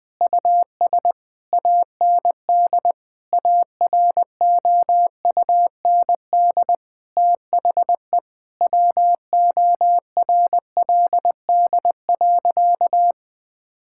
Morse code, heard as a keyed tone, about 20 wpm, 700 Hz. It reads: US AND AROUND THE WORLD.
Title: Sentences with 7-Word Phrases - 20wpm